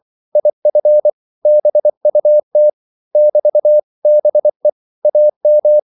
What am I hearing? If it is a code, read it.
Morse code, heard as a keyed tone, about 24 wpm, 600 Hz. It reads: IF BUT <BT>BE AM
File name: CW / Practice / 500MostCommonEnglishWords24